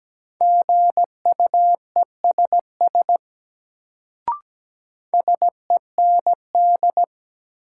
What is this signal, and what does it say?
Morse code, heard as a keyed tone, about 17 wpm, 700 Hz. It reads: GUESS SEND
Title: Sentences from Top 1500 Words - New Word Drill Alt - 17wpm